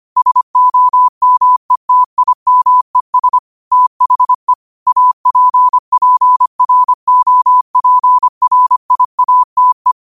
Morse code, heard as a keyed tone, about 25 wpm, 1000 Hz. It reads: SOMETIMES THE APPROPRIATE
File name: qotdfiles / 25wpm